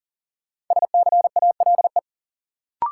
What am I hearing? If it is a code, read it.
Morse code, heard as a keyed tone, about 40 wpm, 700 Hz. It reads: SCALE
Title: Sentences from Top 1000 Words - Review Part 4 - New Word Drill - 40wpm